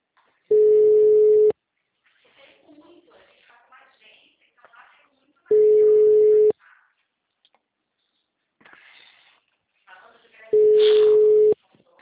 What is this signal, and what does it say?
Ringback tone (Europe, 425 Hz), with speech.